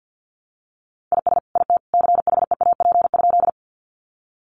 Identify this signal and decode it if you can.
Morse code, heard as a keyed tone, about 50 wpm, 700 Hz. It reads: IS IT CHEAP?
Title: Sentences from Top 1200 Words 50wpm